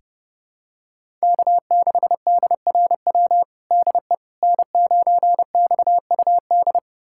Morse code, heard as a keyed tone, about 30 wpm, 700 Hz. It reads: K6DRW DE N9XUD
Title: USA Callsign DE Callsign - 5 Characters - 30wpm